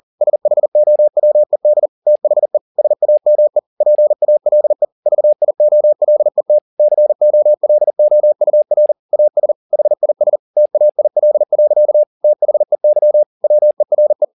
Morse code, heard as a keyed tone, about 40 wpm, 600 Hz. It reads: SHOWED THE SAME PALE VIOLET COLOUR AS HIS TAIL. THEY WERE
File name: the old man and the sea 2 40wpm